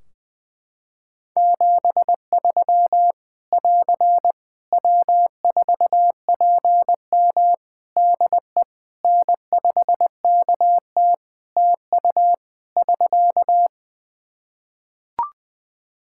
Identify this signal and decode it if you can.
Morse code, heard as a keyed tone, about 20 wpm, 700 Hz. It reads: 73 <AR> W4PM DE N5KT TU <SK>
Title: K1USN SST QSO Practice 20wpm